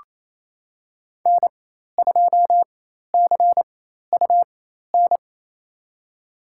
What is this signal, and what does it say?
Morse code, heard as a keyed tone, about 28 wpm, spaced as for 13 wpm, 690 Hz. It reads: N2CUN
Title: USA Call Signs - 5 Characters - 4x Character Spacing - 28wpm